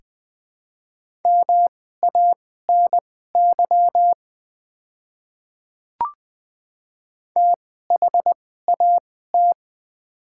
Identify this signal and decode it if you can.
Morse code, heard as a keyed tone, about 20 wpm, spaced as for 15 wpm, 690 Hz. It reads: MANY THAT
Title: 2x Character Spacing - Top 100 Words - 20wpm